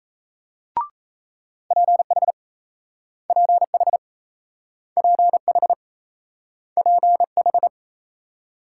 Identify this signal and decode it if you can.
Morse code, heard as a keyed tone, about 38 wpm, 700 Hz. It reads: PH PH PH PH